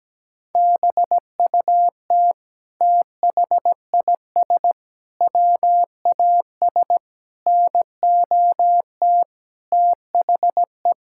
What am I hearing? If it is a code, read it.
Morse code, heard as a keyed tone, about 17 wpm, 700 Hz. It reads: BUT THIS WAS NOT THE
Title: Sentences from Top 1000 Words - Review Part 6 - 17wpm